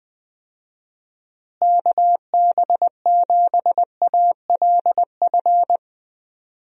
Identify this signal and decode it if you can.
Morse code, heard as a keyed tone, about 20 wpm, 700 Hz. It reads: KB7ALF
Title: USA Call Signs - 6 Characters - Repeated 3x - 20wpm